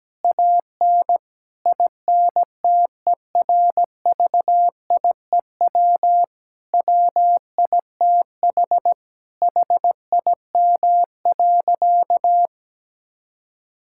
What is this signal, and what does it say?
Morse code, heard as a keyed tone, about 17 wpm, 700 Hz. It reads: AN INTERVIEW WITH HIM.
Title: Sentences from Top 1600 Words 17wpm